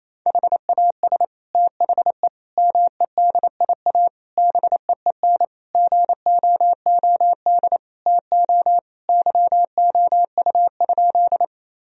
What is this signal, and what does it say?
Morse code, heard as a keyed tone, about 28 wpm, 700 Hz. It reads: HAS THE MEDIA BEEN GOOD TO YOU?
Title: Sentences from Top 1000 Words - Review Part 4 - 28wpm